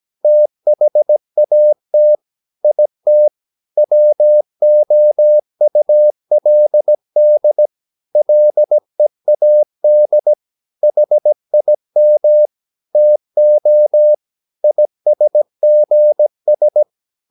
Morse code, heard as a keyed tone, about 17 wpm, 600 Hz. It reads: THAT IT WOULD LEAD HIM TO ISGS
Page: Romanen Never af Ken Follet på engelsk.